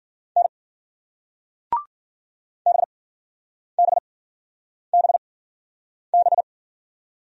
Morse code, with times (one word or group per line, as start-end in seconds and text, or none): 0.4-0.5 N
2.7-6.4 BBBB